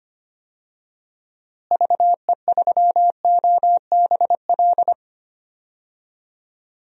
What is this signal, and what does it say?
Morse code, heard as a keyed tone, about 25 wpm, 700 Hz. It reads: VE3OBL
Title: DXCC-100 Call Signs - 6 Characters - Repeated 2x - 25wpm